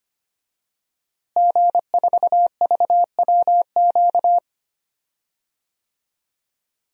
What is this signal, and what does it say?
Morse code, heard as a keyed tone, about 25 wpm, 700 Hz. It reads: G4VWQ
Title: DXCC-100 Call Signs - 5 Characters - Repeated 2x - 25wpm